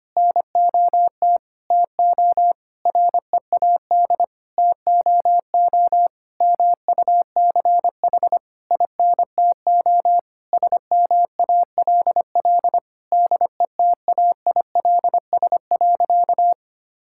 Morse code, heard as a keyed tone, about 25 wpm, 700 Hz. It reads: NOT TO READ TOO MUCH INTO SMALL DETAILS.
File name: practice / 25wpm / Sentences with 9-Word Phrases - 25wpm